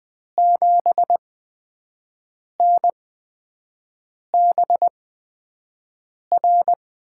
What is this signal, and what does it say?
Morse code, heard as a keyed tone, about 20 wpm, spaced as for 5 wpm, 700 Hz. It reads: 7NBR
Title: USA Call Signs - 6 Characters - 8x Character Spacing - 20wpm